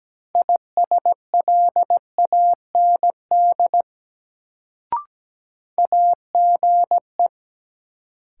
Morse code, heard as a keyed tone, about 17 wpm, 700 Hz. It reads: ISLAND AGE
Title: Sentences from Top 500 Words - New Word Drill - 17wpm